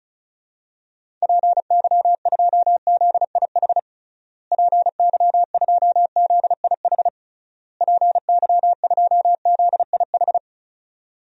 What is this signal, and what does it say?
Morse code, heard as a keyed tone, about 36 wpm, 700 Hz. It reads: PY2ZIH PY2ZIH PY2ZIH
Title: DXCC-100 Call Signs - 6 Characters - Repeated 3x - 35wpm